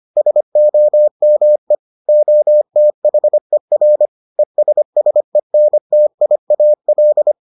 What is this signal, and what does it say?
Morse code, heard as a keyed tone, about 25 wpm, 600 Hz. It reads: SOME OTHER ESSENTIAL